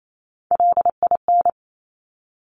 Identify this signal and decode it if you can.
Morse code, heard as a keyed tone, about 28 wpm, 700 Hz. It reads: LIN
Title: Top Three Letter Combinations in Words 28wpm